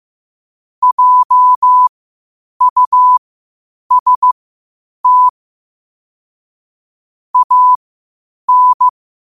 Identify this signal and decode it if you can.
Morse code, heard as a keyed tone, about 15 wpm, spaced as for 8 wpm, 1000 Hz. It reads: JUST AN
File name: qotdfiles / 8wpm